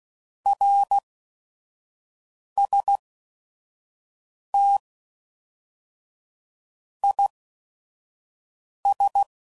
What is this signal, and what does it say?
Morse code, heard as a keyed tone, about 16 wpm, spaced as for 5 wpm, 800 Hz. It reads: RST IS
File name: CW Code Groups @5WPM / QSOs / kc7sac